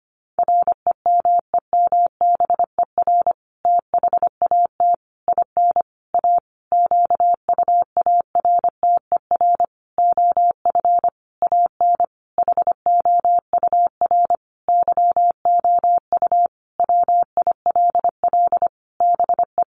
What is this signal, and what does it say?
Morse code, heard as a keyed tone, about 25 wpm, 700 Hz. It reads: REMEMBER THAT IN A QUARTER OF AN HOUR YOU WILL BE
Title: Sentences from Top 1300 Words 25wpm